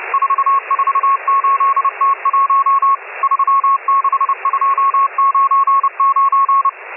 Morse code, heard as a keyed tone, about 30 wpm, 1100 Hz, with radio noise. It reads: 448T1 36299
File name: M12- -1010utc